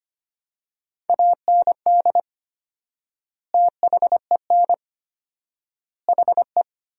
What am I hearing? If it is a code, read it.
Morse code, heard as a keyed tone, about 25 wpm, 700 Hz. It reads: AND THEN HE
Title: 4x Word-Spacing - Sentences from Top 400 Words - 25wpm